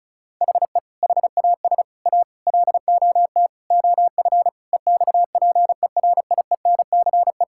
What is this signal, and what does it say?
Morse code, heard as a keyed tone, about 36 wpm, 700 Hz. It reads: HE HAS A LOT OF EXPERIENCE
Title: Sentences from Top 1000 Words - Review Part 6 - 35wpm